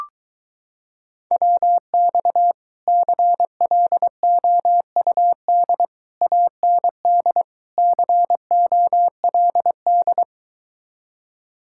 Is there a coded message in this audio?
Yes: Morse code from 1.3 to 10.2 s, heard as a keyed tone, about 24 wpm, 700 Hz. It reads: WX CLOUD AND COLD